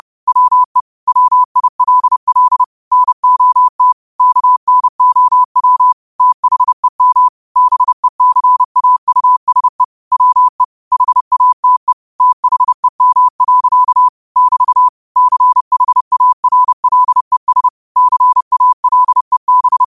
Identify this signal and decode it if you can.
Morse code, heard as a keyed tone, about 30 wpm, 1000 Hz. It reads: WE WILL NOT KNOW THEM BECAUSE WE HATE THEM. <BT> CHARLES CALEB
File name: qotdfiles / 30wpm